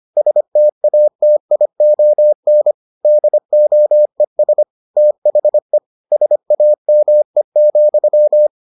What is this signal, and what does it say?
Morse code, heard as a keyed tone, about 25 wpm, 600 Hz. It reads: STATION DOES THE SAME,